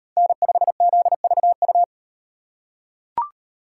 Morse code, heard as a keyed tone, about 38 wpm, 700 Hz. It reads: N5ZVU